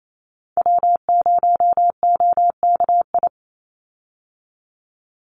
Morse code, heard as a keyed tone, about 28 wpm, 700 Hz. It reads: W0OKI
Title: USA Call Signs 28wpm